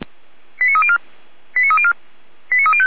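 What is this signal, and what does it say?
Five-tone selective call (ZVEI), heard as quick runs of five tones, tones of about 80 ms. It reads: 89304 89304 89305
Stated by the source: ZVEI five-tone selective call received with an SDR.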